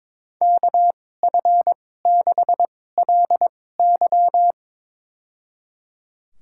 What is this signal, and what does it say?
Morse code, heard as a keyed tone, about 21 wpm, spaced as for 16 wpm, 700 Hz. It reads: KF6LY
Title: USA Call Signs - 5 Characters - 2x Character Spacing - 22wpm